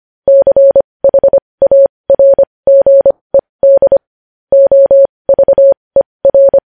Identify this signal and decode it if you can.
Morse code, heard as a keyed tone, about 25 wpm, spaced as for 20 wpm, 550 Hz. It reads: CHARGED OVER
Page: news-headlines-in-morse-code-at-20-wpm